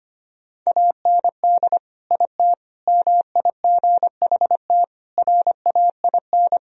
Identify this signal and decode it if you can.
Morse code, heard as a keyed tone, about 25 wpm, 700 Hz. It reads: AND IT MIGHT RAIN
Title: Sentences from Top 1000 Words - Review Part 1 - 25wpm